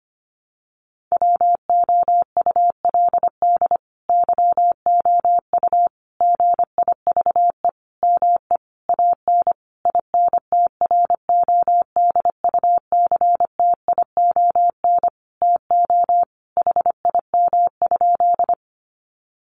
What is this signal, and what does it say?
Morse code, heard as a keyed tone, about 25 wpm, 700 Hz. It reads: WOULD YOU GIVE ME AN INTRODUCTION TO HIM?
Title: Sentences from Top 1000 Words 25wpm